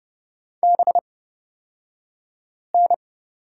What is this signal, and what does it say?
Morse code, heard as a keyed tone, about 30 wpm, spaced as for 5 wpm, 700 Hz. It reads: BN
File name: Lesson 36d - BT - Two Characters 30wpm